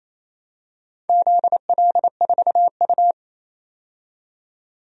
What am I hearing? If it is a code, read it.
Morse code, heard as a keyed tone, about 29 wpm, 700 Hz. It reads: ZL4U
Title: DXCC-100 Call Signs - 4 Characters - 28wpm